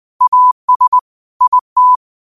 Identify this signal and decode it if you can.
Morse code, heard as a keyed tone, about 20 wpm, 1000 Hz. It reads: AS IT
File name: qotdfiles / 20wpm